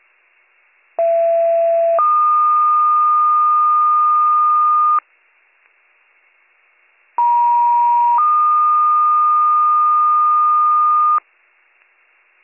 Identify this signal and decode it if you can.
Two-tone page, heard as a one-second tone, then a three-second tone, about 670 then 1220 Hz, about 950 then 1220 Hz. It is heard from 1.0 to 11.2 s.